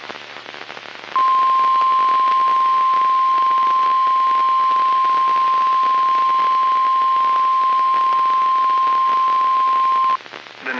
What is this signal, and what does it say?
Emergency Alert System (SAME) tones (the 1050 Hz attention tone). It is heard from 1.1 to 10.2 s.
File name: eas messages / EAS WXL38 Binghamton